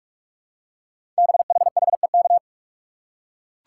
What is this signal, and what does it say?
Morse code, heard as a keyed tone, about 44 wpm, 700 Hz. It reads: BHHEK
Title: Code Groups - 5 Letters - 45wpm